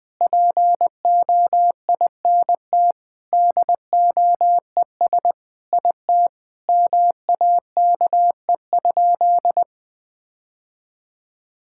Morse code, heard as a keyed tone, about 21 wpm, 700 Hz. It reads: POINT DOES IT MAKE?
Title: Sentences with 8-Word Phrases - 20wpm